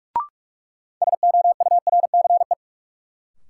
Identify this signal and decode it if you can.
Morse code, heard as a keyed tone, about 42 wpm, 700 Hz. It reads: SOURCE
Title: Sentences from Top 800 Words - New Word Drill - 45wpm